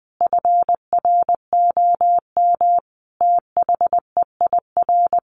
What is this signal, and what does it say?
Morse code, heard as a keyed tone, about 20 wpm, 700 Hz. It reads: FROM THEIR
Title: A Sound Of Thunder 20wpm